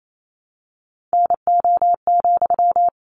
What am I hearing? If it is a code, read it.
Morse code, heard as a keyed tone, about 28 wpm, 700 Hz. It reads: NO,
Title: Sentences from Top 1200 Words 28wpm